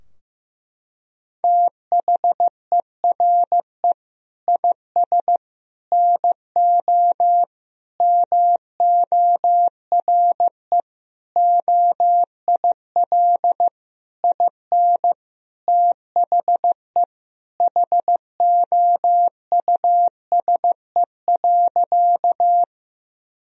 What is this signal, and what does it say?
Morse code, heard as a keyed tone, about 15 wpm, 700 Hz. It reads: THERE IS NO MORE OIL IN THE HOUSE.
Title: Sentences from Top 1000 Words - Review Part 4 - 15wpm